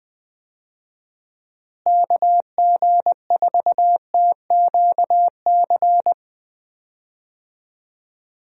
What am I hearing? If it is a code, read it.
Morse code, heard as a keyed tone, about 20 wpm, 700 Hz. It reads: KG4TQC
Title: USA Call Signs - 6 Characters - Repeated 2x - 20wpm